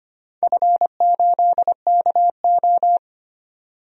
Morse code, heard as a keyed tone, about 25 wpm, 700 Hz. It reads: F8KO
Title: DXCC-100 Call Signs - 4 Characters - 25wpm